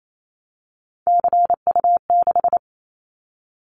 Morse code, heard as a keyed tone, about 28 wpm, 700 Hz. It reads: CU6
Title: Callsign Prefixes for Top 50 DXCC - 28wpm